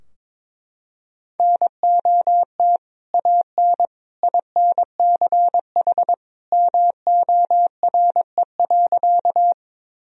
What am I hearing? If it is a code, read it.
Morse code, heard as a keyed tone, about 22 wpm, 700 Hz. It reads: NOT AN INCH MORE.
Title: Sentences from Top 1000 Words - Review Part 3 - 22wpm